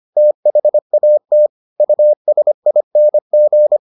Morse code, heard as a keyed tone, about 25 wpm, 600 Hz. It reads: THAT USING